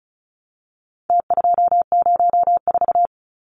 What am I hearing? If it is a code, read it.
Morse code, heard as a keyed tone, about 34 wpm, 700 Hz. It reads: T204
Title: Numbers - Contest Style - 4-Digit 35wpm